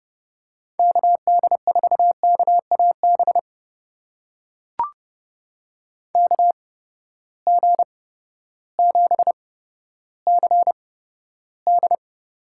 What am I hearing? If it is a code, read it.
Morse code, heard as a keyed tone, about 30 wpm, spaced as for 14 wpm, 700 Hz. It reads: KD4KAB KG7CD
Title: USA Call Signs - 6 Characters - 8x Character Spacing - 30wpm